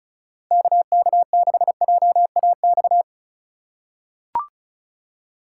Morse code, heard as a keyed tone, about 34 wpm, 700 Hz. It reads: KK6JAX
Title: USA Call Signs - 6 Characters - 2x Character Spacing - 35wpm